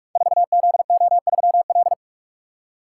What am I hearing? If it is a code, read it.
Morse code, heard as a keyed tone, about 44 wpm, 690 Hz. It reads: 4ZO3L
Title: Code Groups - 5 Letter:Numbers - 45wpm